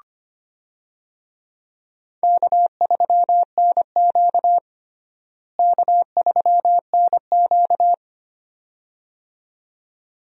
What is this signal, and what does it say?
Morse code, heard as a keyed tone, about 25 wpm, 700 Hz. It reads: K3NQ K3NQ
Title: USA Call Signs - 4 Characters - Repeated 2x - 25wpm